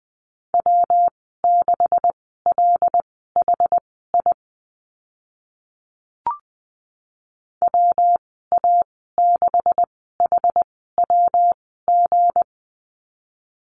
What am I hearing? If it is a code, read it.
Morse code, heard as a keyed tone, about 20 wpm, spaced as for 14 wpm, 700 Hz. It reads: W6LHI WA6HWG
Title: 179 Lesson 29g - Slash - Call Signs 20wpm